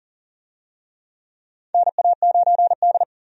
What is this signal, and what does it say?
Morse code, heard as a keyed tone, about 40 wpm, 700 Hz. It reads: NA9D
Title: USA Call Signs - 4 Characters - 40wpm